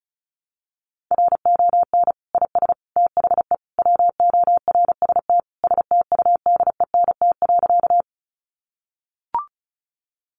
Morse code, heard as a keyed tone, about 34 wpm, 700 Hz. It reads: RON IS THE WORST STUDENT.
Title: Additional Easy Phrases 35wpm